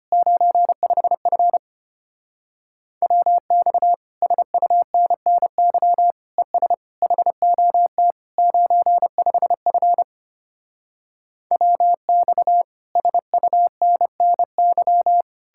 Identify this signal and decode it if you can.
Morse code, heard as a keyed tone, about 33 wpm, 700 Hz. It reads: 95F WX SUNNY ES HOT 95F WX SUNNY